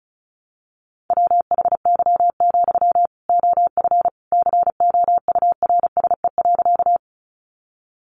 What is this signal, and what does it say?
Morse code, heard as a keyed tone, about 34 wpm, 700 Hz. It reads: WHY, OF COURSE.
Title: Sentences from Top 500 Words 35wpm